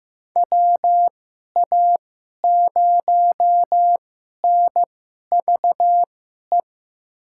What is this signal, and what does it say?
Morse code, heard as a keyed tone, about 15 wpm, spaced as for 11 wpm, 700 Hz. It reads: WA0NVE
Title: USA Call Signs - 6 Characters - 2x Character Spacing - 15wpm